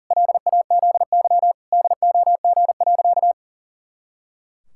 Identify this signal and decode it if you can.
Morse code, heard as a keyed tone, about 40 wpm, 700 Hz. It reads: LAZY DOG.
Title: Pangram Challenge v2 - 40wpm